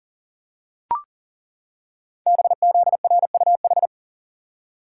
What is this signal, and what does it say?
Morse code, heard as a keyed tone, about 40 wpm, 700 Hz. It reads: BZRUH
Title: Code Groups - 5 Letters - 40wpm